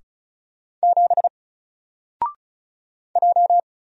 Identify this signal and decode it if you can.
Morse code, heard as a keyed tone, about 36 wpm, 700 Hz. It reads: 7 J
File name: Single-Letter-Number-35wpm